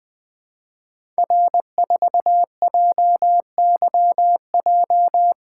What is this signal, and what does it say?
Morse code, heard as a keyed tone, about 20 wpm, 700 Hz. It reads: R4JYJ
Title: Code Groups - 5 Letter:Numbers - 20wpm